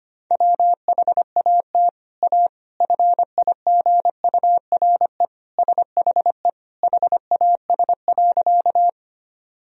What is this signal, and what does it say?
Morse code, heard as a keyed tone, about 25 wpm, 700 Hz. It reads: WHAT A FIGURE SHE HAS.
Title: Sentences from Top 1000 Words - Review Part 2 - 25wpm